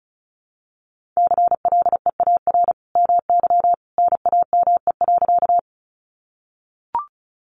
Morse code, heard as a keyed tone, about 34 wpm, 700 Hz. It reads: CLEAR MY NAME.
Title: Sentences from Top 500 Words 35wpm